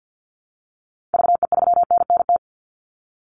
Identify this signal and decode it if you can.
Morse code, heard as a keyed tone, about 48 wpm, 700 Hz. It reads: VE3NNT
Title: Call Signs Repeated 3x - 50wpm